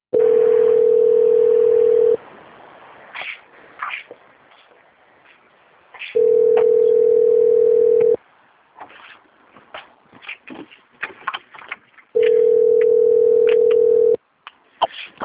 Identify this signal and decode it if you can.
Ringback tone (North America), 440 + 480 Hz.
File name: out- - - - - .43042